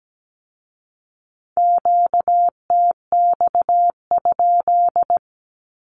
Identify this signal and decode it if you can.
Morse code, heard as a keyed tone, about 17 wpm, 700 Hz. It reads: QTX?
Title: Q-Codes with Spoken Meaning - 17wpm